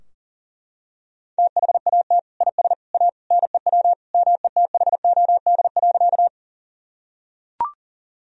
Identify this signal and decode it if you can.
Morse code, heard as a keyed tone, about 40 wpm, 700 Hz. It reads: THAT IS A NEW METHOD.
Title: Sentences from Top 1000 Words - Review Part 3 - 40wpm